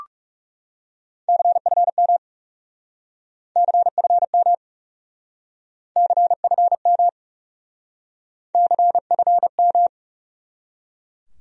Morse code, heard as a keyed tone, about 37 wpm, 700 Hz. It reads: CFM CFM CFM CFM
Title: Speed-Racing Abbreviations - 30wpm